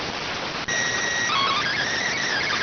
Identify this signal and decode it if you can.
SSTV (slow-scan television), PD 290.